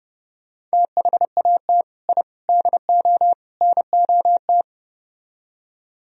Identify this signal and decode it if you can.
Morse code, heard as a keyed tone, about 32 wpm, 700 Hz. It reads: THAT I DO NOT